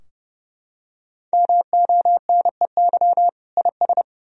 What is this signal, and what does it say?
Morse code, heard as a keyed tone, about 32 wpm, 700 Hz. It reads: MONEY IS
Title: Sentences from Top 1000 Words - Review Part 4 - 30wpm